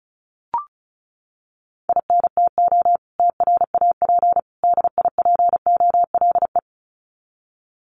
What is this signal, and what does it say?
Morse code, heard as a keyed tone, about 34 wpm, 700 Hz. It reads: INTO TRAP DIPOLE
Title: Common QSO Phrases 35wpm